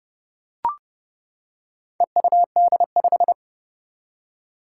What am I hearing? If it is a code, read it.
Morse code, heard as a keyed tone, about 29 wpm, 700 Hz. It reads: EUD5
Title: Four Challenging Letter-Number Combinations 30wpm